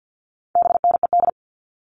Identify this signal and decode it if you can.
Morse code, heard as a keyed tone, about 50 wpm, 700 Hz. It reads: BNED
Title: All Cut Numbers - 4-Digit 50wpm